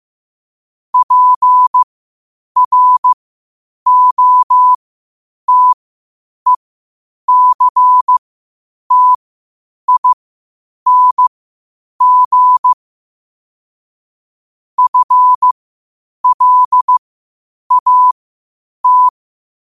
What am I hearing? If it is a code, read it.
Morse code, heard as a keyed tone, about 15 wpm, spaced as for 8 wpm, 1000 Hz. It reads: PROTECTING FLAT
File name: qotdfiles / 8wpm